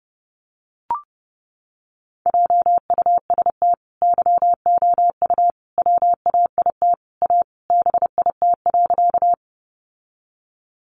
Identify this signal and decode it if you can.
Morse code, heard as a keyed tone, about 30 wpm, 700 Hz. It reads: JUST YOU WAIT A BIT.
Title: Sentences from Top 900 Words 30wpm